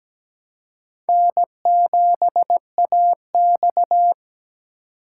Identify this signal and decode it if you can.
Morse code, heard as a keyed tone, about 20 wpm, 700 Hz. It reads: N7AX
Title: Speed Racing - USA 1x2 Call Signs 17wpm